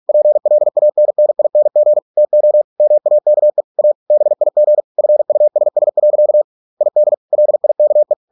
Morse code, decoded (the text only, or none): PLANNING TO MAKE A BIG FUSS. ID LIKE